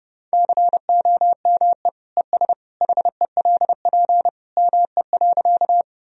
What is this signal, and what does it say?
Morse code, heard as a keyed tone, about 31 wpm, 700 Hz. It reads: COME ES HELP ME.
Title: Sentences from Top 200 Words and Abbreviations 30wpm